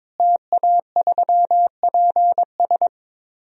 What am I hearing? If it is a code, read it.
Morse code, heard as a keyed tone, about 22 wpm, 700 Hz. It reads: TA3PS
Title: DXCC-100 Call Signs - 5 Characters - Repeated 3x - 22wpm